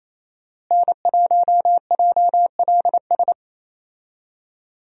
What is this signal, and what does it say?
Morse code, heard as a keyed tone, about 28 wpm, 700 Hz. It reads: N1JLS
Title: USA Call Signs - 5 Characters - Repeated 2x - 28wpm